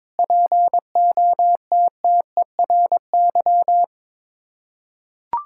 Morse code, heard as a keyed tone, about 22 wpm, 700 Hz. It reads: POTTERY
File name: Difficult Words with Hints - 22wpm